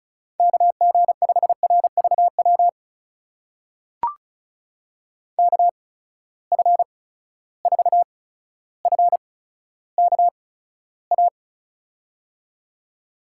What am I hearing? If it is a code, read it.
Morse code, heard as a keyed tone, about 36 wpm, spaced as for 15 wpm, 700 Hz. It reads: KG5RVW KF4FKA
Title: USA Call Signs - 6 Characters - 8x Character Spacing - 35wpm